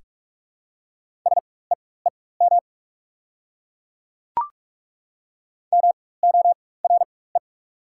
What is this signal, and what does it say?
Morse code, heard as a keyed tone, about 42 wpm, spaced as for 21 wpm, 700 Hz. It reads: SEEM MORE
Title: 4x Character Spacing - Top 300 Words - 45wpm